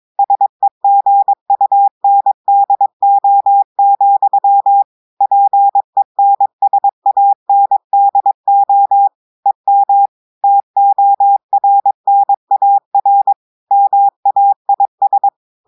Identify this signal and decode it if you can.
Morse code, heard as a keyed tone, about 22 wpm, 800 Hz. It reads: SEGUNDO, PENSANDO EM TORNAR MAIS
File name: audios / portugais / portugais texte en clair 1 vitesse 22 mots minutes